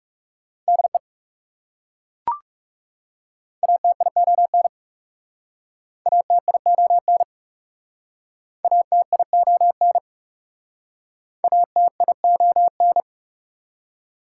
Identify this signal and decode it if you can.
Morse code, heard as a keyed tone, about 42 wpm, 690 Hz. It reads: DE ATION ATION ATION ATION